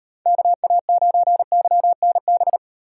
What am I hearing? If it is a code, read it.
Morse code, heard as a keyed tone, about 38 wpm, 700 Hz. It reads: KA9YNB